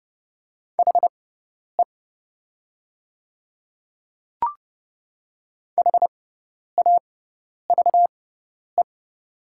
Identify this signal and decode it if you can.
Morse code, heard as a keyed tone, about 30 wpm, spaced as for 10 wpm, 700 Hz. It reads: HE HAVE
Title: 6x Character Spacing - Top 100 Words - 30wpm